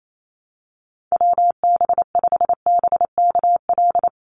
Morse code, heard as a keyed tone, about 28 wpm, 700 Hz. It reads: WB5BKL
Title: NA SOTA Chaser Callsigns 28wpm